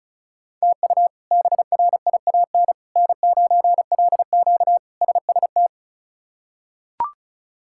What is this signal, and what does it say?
Morse code, heard as a keyed tone, about 36 wpm, 700 Hz. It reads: TU BRIAN N9LQ SST
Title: K1USN SST QSO Practice - 35wpm